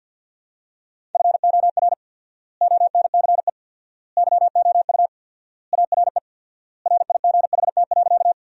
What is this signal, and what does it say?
Morse code, heard as a keyed tone, about 48 wpm, 700 Hz. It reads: FOR ONCE YOU ARE RIGHT.
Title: 4x Word-Spacing - Sentences from Top 300 Words - 50wpm